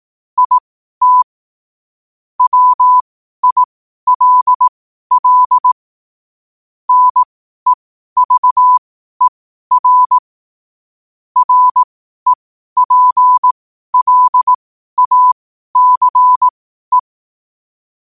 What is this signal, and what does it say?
Morse code, heard as a keyed tone, about 18 wpm, spaced as for 12 wpm, 1000 Hz. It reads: IT WILL NEVER REPLACE
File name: qotdfiles / 12wpm